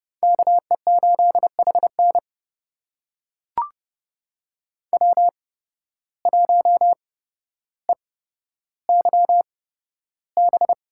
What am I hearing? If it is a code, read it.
Morse code, heard as a keyed tone, about 30 wpm, spaced as for 13 wpm, 700 Hz. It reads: KE8HN W1EYB
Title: USA Call Signs - 5 Characters - 8x Character Spacing - 30wpm